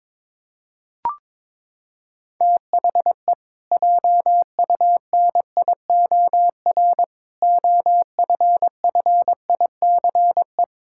Morse code, heard as a keyed tone, about 22 wpm, 700 Hz. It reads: THE JUNIOR OFFICE
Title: Pangram Challenge v2 - 22wpm